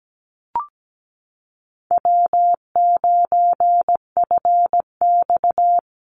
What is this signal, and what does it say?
Morse code, heard as a keyed tone, about 17 wpm, 700 Hz. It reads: W9FX
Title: Easy Call Signs Repeated 2x - 17wpm